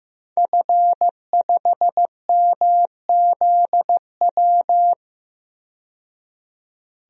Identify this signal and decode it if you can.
Morse code, heard as a keyed tone, about 15 wpm, 700 Hz. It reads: F5MZW
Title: DXCC-100 Call Signs - 5 Characters - 15wpm